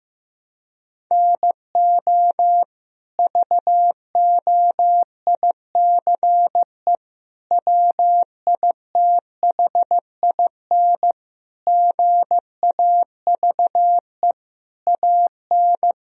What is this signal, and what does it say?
Morse code, heard as a keyed tone, about 15 wpm, 700 Hz. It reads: NO VOICE WITHIN GAVE AN
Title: Sentences from Top 1000 Words - Review Part 4 - 15wpm